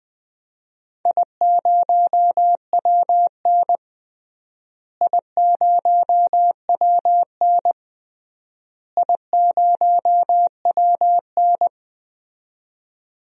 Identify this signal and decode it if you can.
Morse code, heard as a keyed tone, about 21 wpm, 700 Hz. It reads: I0WN I0WN I0WN
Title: DXCC-100 Call Signs - 4 Characters - Repeated 3x - 20wpm